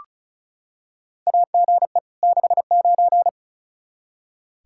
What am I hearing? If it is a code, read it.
Morse code, heard as a keyed tone, about 36 wpm, 700 Hz. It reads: AGE 69
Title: Practice QSO - Exchange 3 - More Info - 35wpm